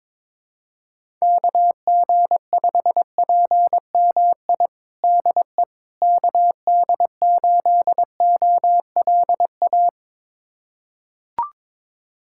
Morse code, heard as a keyed tone, about 22 wpm, 700 Hz. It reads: KG5PMI DE KD8OLA
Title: USA Callsign DE Callsign - 6 Characters - 22wpm